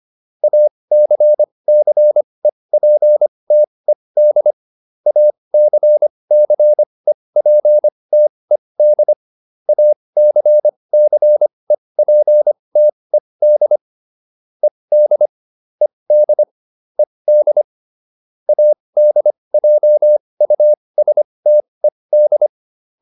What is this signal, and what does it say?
Morse code, heard as a keyed tone, about 25 wpm, spaced as for 20 wpm, 600 Hz. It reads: ACCEPTED ACCEPTED ACCEPTED ED ED ED ADJUSTED
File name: ED8 20